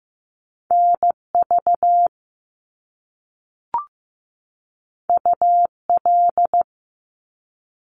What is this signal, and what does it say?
Morse code, heard as a keyed tone, about 15 wpm, 700 Hz. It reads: NV UL
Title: Top Two Letter Combinations in Words 15wpm